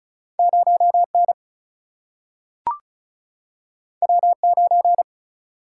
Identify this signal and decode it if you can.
Morse code, heard as a keyed tone, about 36 wpm, 700 Hz. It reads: 0 N W 9
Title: Single-Letter Single-Number 35wpm